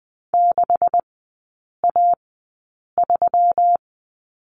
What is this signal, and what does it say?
Morse code, heard as a keyed tone, about 20 wpm, spaced as for 8 wpm, 700 Hz. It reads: 6A3
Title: 127 Lesson 21e - Number 6 - Three Characters 20wpm